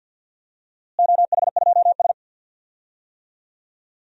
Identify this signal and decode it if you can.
Morse code, heard as a keyed tone, about 46 wpm, 700 Hz. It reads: OH2S
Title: DXCC-100 Call Signs - 4 Characters - Repeated 2x - 50wpm